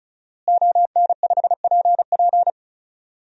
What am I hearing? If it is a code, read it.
Morse code, heard as a keyed tone, about 34 wpm, 700 Hz. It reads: ON5PP
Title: DXCC-100 Call Signs - 5 Characters - Repeated 2x - 35wpm